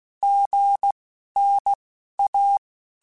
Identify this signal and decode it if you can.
Morse code, heard as a keyed tone, about 16 wpm, 800 Hz. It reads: G N A
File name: Basic Letter Groups By7Groups / AWUNDG / AWUNDG@15